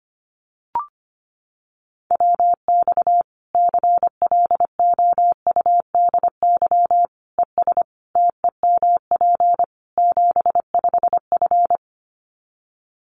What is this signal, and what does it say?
Morse code, heard as a keyed tone, about 25 wpm, 700 Hz. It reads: WX CLOUDY ES TEMP 75F
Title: Common QSO Phrases 25wpm